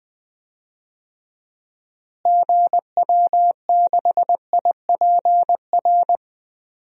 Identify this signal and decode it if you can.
Morse code, heard as a keyed tone, about 19 wpm, 700 Hz. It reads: GW6IPR
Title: DXCC-100 Call Signs - 6 Characters - Repeated 3x - 20wpm